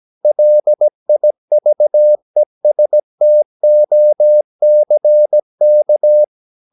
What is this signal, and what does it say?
Morse code, heard as a keyed tone, about 17 wpm, 600 Hz. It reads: LIVESTOCK
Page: Romanen Never af Ken Follet på engelsk.